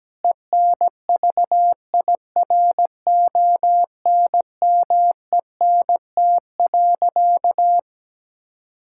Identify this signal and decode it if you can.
Morse code, heard as a keyed tone, about 17 wpm, 700 Hz. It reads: ENVIRONMENT.
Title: Sentences from Top 1000 Words - Review Part 5 - 17wpm